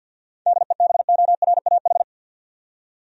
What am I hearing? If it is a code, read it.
Morse code, heard as a keyed tone, about 48 wpm, 700 Hz. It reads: DEBORAH
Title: USA Common Names - 50wpm